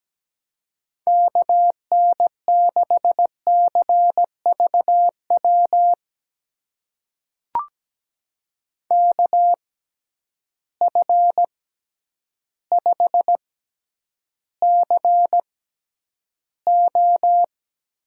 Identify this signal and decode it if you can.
Morse code, heard as a keyed tone, about 17 wpm, spaced as for 9 wpm, 700 Hz. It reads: KN6CVW KF5CO
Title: USA Call Signs - 6 Characters - 6x Character Spacing - 17wpm